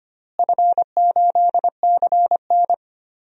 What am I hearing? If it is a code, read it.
Morse code, heard as a keyed tone, about 25 wpm, 700 Hz. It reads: F8CN
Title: DXCC-100 Call Signs - 4 Characters - Repeated 3x - 25wpm